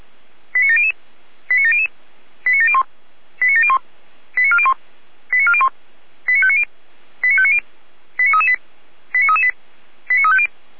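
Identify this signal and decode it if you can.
Five-tone selective call (ZVEI), heard as quick runs of five tones, tones of about 80 ms. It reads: 89700 89700 89701 89701 89401 89401 89590 89590 89338 89338 89350
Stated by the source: ZVEI five-tone selective call code 89104, received by SDR, from the Feuerwehr Orlamünde fire brigade audio server.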